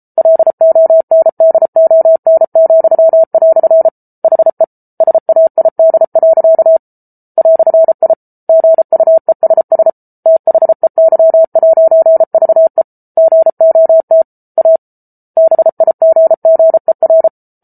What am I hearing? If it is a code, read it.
Morse code, heard as a keyed tone, about 34 wpm, 650 Hz. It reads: LONDON," HE SAID. "I GUESS THEY'VE GOT A BIGGER